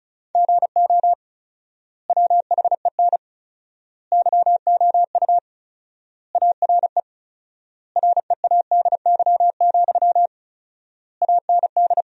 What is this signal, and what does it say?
Morse code, heard as a keyed tone, about 36 wpm, 700 Hz. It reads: GO WHEN YOU ARE READY, AND
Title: 4x Word-Spacing - Sentences from Top 500 Words - 35wpm